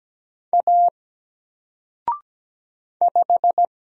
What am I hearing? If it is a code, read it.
Morse code, heard as a keyed tone, about 17 wpm, 700 Hz. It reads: A 5
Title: Single Character (Letter:Number:Punctuation:Prosign) Alt 17wpm